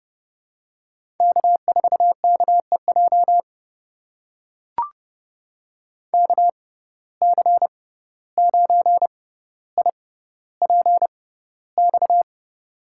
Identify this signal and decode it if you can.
Morse code, heard as a keyed tone, about 29 wpm, spaced as for 14 wpm, 700 Hz. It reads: K4KEJ KC9IPX
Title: USA Call Signs - 6x Character Spacing - 30wpm